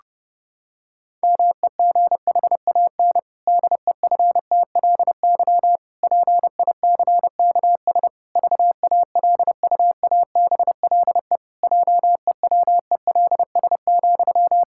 Morse code, heard as a keyed tone, about 30 wpm, 700 Hz. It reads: MEGHAN DEFTLY PICKS VALUABLE JEWELS,
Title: Pangram Challenge v2 - 30wpm